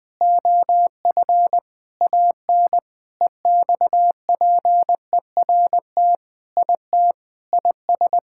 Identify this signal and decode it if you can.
Morse code, heard as a keyed tone, about 20 wpm, 700 Hz. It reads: OF AN EXPERT IT IS
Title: Sentences with 6-Word Phrases - 20wpm